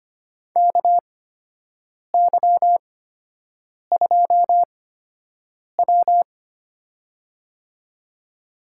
Morse code, heard as a keyed tone, about 25 wpm, spaced as for 7 wpm, 700 Hz. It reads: KY2W
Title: USA Call Signs - 4 Characters - 8x Character Spacing - 25wpm